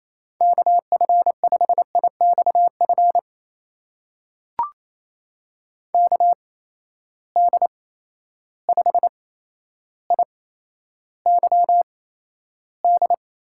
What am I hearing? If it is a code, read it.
Morse code, heard as a keyed tone, about 28 wpm, spaced as for 12 wpm, 700 Hz. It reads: KF5IXF KD5IYD
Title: USA Call Signs - 6 Characters - 8x Character Spacing - 28wpm